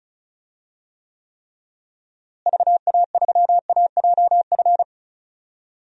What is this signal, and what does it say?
Morse code, heard as a keyed tone, about 34 wpm, 700 Hz. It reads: VA3AJF
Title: DXCC-100 Call Signs - 6 Characters - Repeated 3x - 35wpm